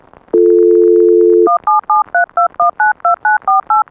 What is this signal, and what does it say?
DTMF, heard as telephone touch tones, tones of about 150 ms. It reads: 17732192948
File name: music / ipaghost weeklybeats-2022 2 you-got-me-phreaking